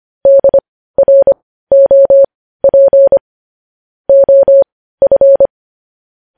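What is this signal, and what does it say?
Morse code, heard as a keyed tone, about 25 wpm, spaced as for 15 wpm, 550 Hz. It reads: DROP OF
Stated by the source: Morse news headlines practice sent at 15 WPM.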